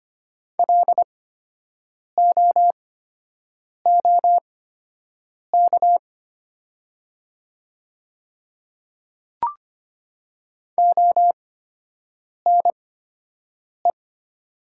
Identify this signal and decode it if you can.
Morse code, heard as a keyed tone, about 25 wpm, spaced as for 7 wpm, 700 Hz. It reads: LOOK ONE
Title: 8x Character Spacing - Top 100 Words - 25wpm